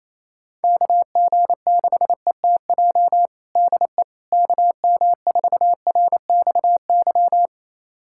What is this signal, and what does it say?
Morse code, heard as a keyed tone, about 28 wpm, 700 Hz. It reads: KG6ETJ DE KM4RXY
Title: USA Callsign DE Callsign 28wpm